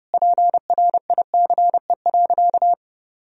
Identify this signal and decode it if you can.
Morse code, heard as a keyed tone, about 30 wpm, 700 Hz. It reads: PRICE.
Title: Sentences from Top 1000 Words - Review Part 4 - 30wpm